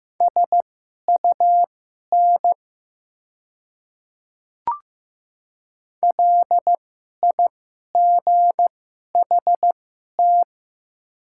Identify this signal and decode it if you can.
Morse code, heard as a keyed tone, about 15 wpm, spaced as for 11 wpm, 700 Hz. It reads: SUN LIGHT
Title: 2x Character Spacing - Top 500 Words - 15wpm